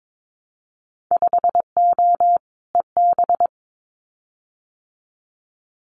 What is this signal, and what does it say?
Morse code, heard as a keyed tone, about 22 wpm, 700 Hz. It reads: 5O EB
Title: Winter Field Day Exchanges Encore 22wpm